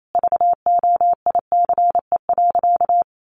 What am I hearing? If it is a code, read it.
Morse code, heard as a keyed tone, about 28 wpm, 700 Hz. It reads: VOICE.
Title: Sentences from Top 1500 Words 28wpm